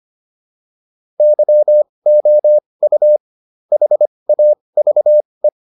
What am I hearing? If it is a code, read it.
Morse code, heard as a keyed tone, about 25 wpm, spaced as for 20 wpm, 600 Hz. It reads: YOU HAVE
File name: WD304 20